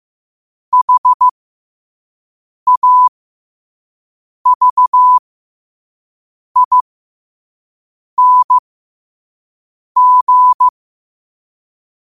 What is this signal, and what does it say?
Morse code, heard as a keyed tone, about 15 wpm, spaced as for 5 wpm, 1000 Hz. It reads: HAVING
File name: qotdfiles / 5wpm